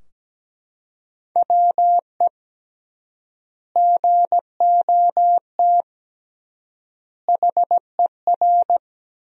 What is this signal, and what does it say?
Morse code, heard as a keyed tone, about 17 wpm, 700 Hz. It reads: WE GOT HER
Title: 3x Word-Spacing - Sentences from Top 400 Words - 17wpm